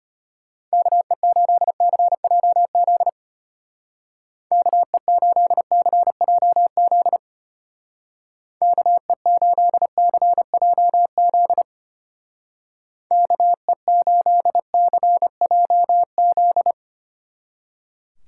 Morse code, heard as a keyed tone, about 32 wpm, 700 Hz. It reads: KE8CJZ KE8CJZ KE8CJZ KE8CJZ